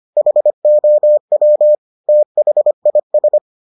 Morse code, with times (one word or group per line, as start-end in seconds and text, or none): 0.2-1.7 HOW
2.1-3.4 THIS